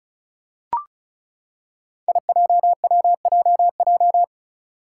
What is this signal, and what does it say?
Morse code, heard as a keyed tone, about 34 wpm, 700 Hz. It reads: IJWJJ
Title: Five Challenging Letter-Number Combinations 35wpm